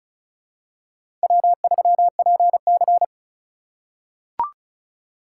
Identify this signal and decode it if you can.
Morse code, heard as a keyed tone, about 36 wpm, 700 Hz. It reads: W3PC
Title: USA Call Signs - 4 Characters - 6x Character Spacing - 35wpm